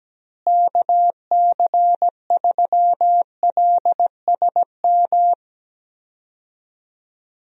Morse code, heard as a keyed tone, about 17 wpm, 700 Hz. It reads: KC3LSM
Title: USA Call Signs - 6 Characters - Repeated 2x - 17wpm